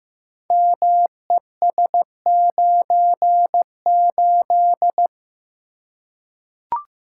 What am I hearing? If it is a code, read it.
Morse code, heard as a keyed tone, about 15 wpm, 700 Hz. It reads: MES98
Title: Code Groups - 5 Letter:Numbers - 15wpm